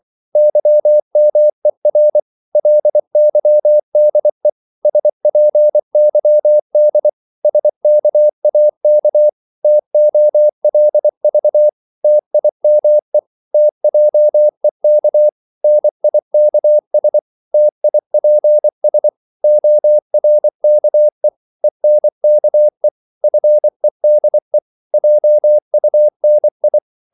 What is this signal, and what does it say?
Morse code, heard as a keyed tone, about 24 wpm, 600 Hz. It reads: YMER LYDE SPYD SKAK TOLV TIME TJEK NIKS TIPS ORKE ENKE FEDE JUNI
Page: Korte danske ord 24wpm